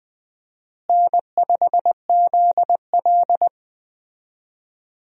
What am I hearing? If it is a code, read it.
Morse code, heard as a keyed tone, about 20 wpm, 700 Hz. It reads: N5ZL
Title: USA Call Signs - 4 Characters - Repeated 2x - 20wpm